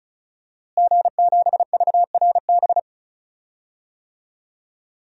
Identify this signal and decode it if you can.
Morse code, heard as a keyed tone, about 34 wpm, 700 Hz. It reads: G7VRB
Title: DXCC-100 Call Signs - 5 Characters - Repeated 3x - 35wpm